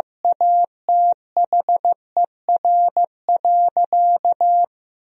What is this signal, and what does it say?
Morse code, heard as a keyed tone, about 15 wpm, 700 Hz. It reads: ATHER.
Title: 4x Word-Spacing - Sentences from Top 300 Words - 15wpm